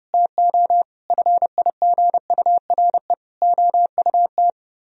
Morse code, heard as a keyed tone, about 30 wpm, 700 Hz. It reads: TO FIGURE OUT